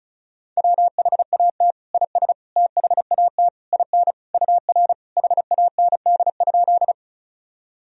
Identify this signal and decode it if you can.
Morse code, heard as a keyed tone, about 36 wpm, 700 Hz. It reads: WHAT IS THAT IN UR HAND?
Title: Sentences from Top 200 Words and Abbreviations 35wpm